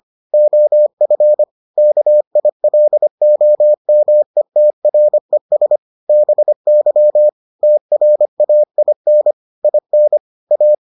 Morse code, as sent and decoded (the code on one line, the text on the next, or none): --- ..-. / -.- .. .-.. --- -- . - .-. ... / -... -.-- / - .-. .- .. -. / .. -. / .-
OF KILOMETRES BY TRAIN IN A